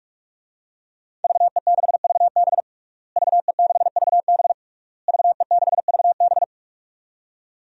Morse code, heard as a keyed tone, about 46 wpm, 700 Hz. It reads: VE6VB VE6VB VE6VB
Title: DXCC-100 Call Signs - 5 Characters - Repeated 3x - 45wpm